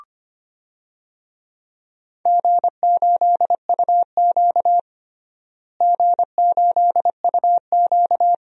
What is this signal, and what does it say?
Morse code, heard as a keyed tone, about 25 wpm, 700 Hz. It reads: G8UQ G8UQ
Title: DXCC-100 Call Signs - 4 Characters - Repeated 2x - 25wpm